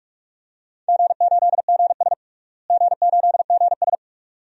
Morse code, heard as a keyed tone, about 46 wpm, 700 Hz. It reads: G8GS G8GS
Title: DXCC-100 Call Signs - 4 Characters - Repeated 2x - 45wpm